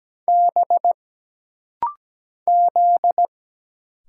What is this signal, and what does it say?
Morse code, heard as a keyed tone, about 17 wpm, 700 Hz. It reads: B Z
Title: Single Letters - Rapid-Fire 17wpm